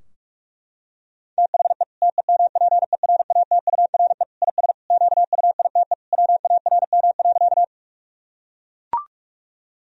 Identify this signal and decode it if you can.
Morse code, heard as a keyed tone, about 41 wpm, 700 Hz. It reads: THE TEMPERATURE IS QUITE WARM.
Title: Sentences from Top 1000 Words - Review Part 4 - 45wpm